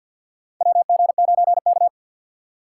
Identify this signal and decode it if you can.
Morse code, heard as a keyed tone, about 53 wpm, 700 Hz. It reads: WG9K
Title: USA Call Signs - 4 Characters - 2x Character Spacing - 50wpm